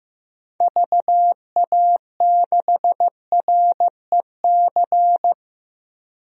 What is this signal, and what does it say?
Morse code, heard as a keyed tone, about 15 wpm, 700 Hz. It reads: VA6REC
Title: DXCC-100 Call Signs - 6 Characters - 15wpm